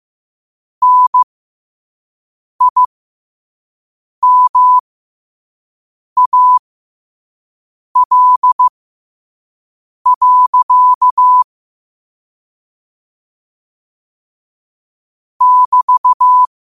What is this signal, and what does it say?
Morse code, heard as a keyed tone, about 15 wpm, spaced as for 5 wpm, 1000 Hz. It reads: NIMAL. <BT>
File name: 5wpm